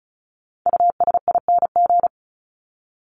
Morse code, heard as a keyed tone, about 34 wpm, 700 Hz. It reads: USING
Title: ING Suffix - Top 100 Words with Base Words 35wpm